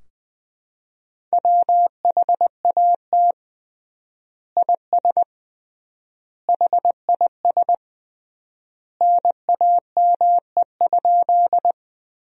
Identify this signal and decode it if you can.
Morse code, heard as a keyed tone, about 21 wpm, 700 Hz. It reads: WHAT IS HIS NAME?
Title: 3x Word-Spacing - Sentences from Top 200 Words - 20wpm